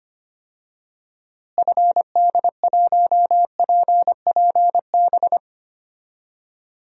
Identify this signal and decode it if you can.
Morse code, heard as a keyed tone, about 25 wpm, 700 Hz. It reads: FD1PPB
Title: DXCC-100 Call Signs - 6 Characters - Repeated 3x - 25wpm